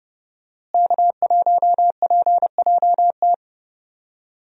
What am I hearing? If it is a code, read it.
Morse code, heard as a keyed tone, about 31 wpm, 700 Hz. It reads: K1PJT
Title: USA Call Signs - 5 Characters - Repeated 3x - 30wpm